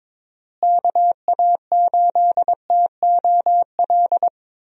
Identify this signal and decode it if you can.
Morse code, heard as a keyed tone, about 22 wpm, 700 Hz. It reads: KA8TOL
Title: USA Call Signs - 6 Characters - Repeated 3x - 22wpm